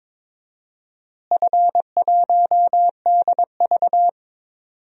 Morse code, heard as a keyed tone, about 22 wpm, 700 Hz. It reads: F1DV
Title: DXCC-100 Call Signs - 4 Characters - Repeated 2x - 22wpm